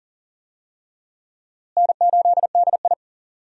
Morse code, heard as a keyed tone, about 37 wpm, 700 Hz. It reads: N8DI
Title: USA Call Signs - 4 Characters - Repeated 3x - 40wpm